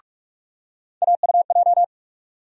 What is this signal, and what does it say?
Morse code, heard as a keyed tone, about 47 wpm, 700 Hz. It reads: AUJ
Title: USA Call Sign Postfixes - 3 Letters 45wpm